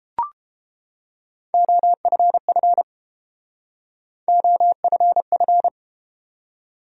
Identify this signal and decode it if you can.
Morse code, heard as a keyed tone, about 31 wpm, 700 Hz. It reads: OFF OFF